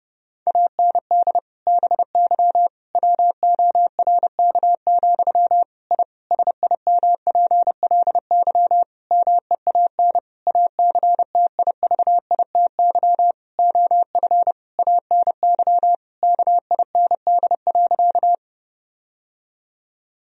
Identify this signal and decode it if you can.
Morse code, heard as a keyed tone, about 30 wpm, 700 Hz. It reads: AND BY WORK, I SIMPLY MEAN ACTIVITY OF ANY KIND.
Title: Sentences from Top 1000 Words - Review Part 4 - 30wpm